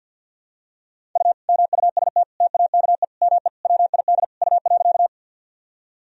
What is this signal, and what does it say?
Morse code, heard as a keyed tone, about 51 wpm, 700 Hz. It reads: U MUST TAKE ME WID U.
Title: Sentences from Top 200 Words and Abbreviations 50wpm